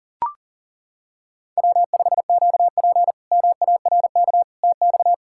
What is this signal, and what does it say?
Morse code, heard as a keyed tone, about 37 wpm, 700 Hz. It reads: W5QP MARK TX
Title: K1USN SST QSO Practice - 40wpm